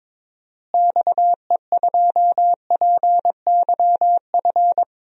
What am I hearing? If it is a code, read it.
Morse code, heard as a keyed tone, about 22 wpm, 700 Hz. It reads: XE2PYF
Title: DXCC-100 Call Signs - 6 Characters - Repeated 2x - 22wpm